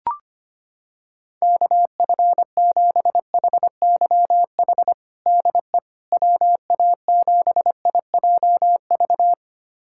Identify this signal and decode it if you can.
Morse code, heard as a keyed tone, about 25 wpm, 700 Hz. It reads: KF7HYH DE WA7IJV
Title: USA Callsign DE Callsign - 6 Characters - 25wpm